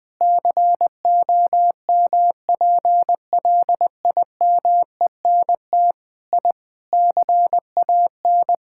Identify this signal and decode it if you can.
Morse code, heard as a keyed tone, about 20 wpm, 700 Hz. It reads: COMPLIMENT I CAN
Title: Sentences with 8-Word Phrases - 20wpm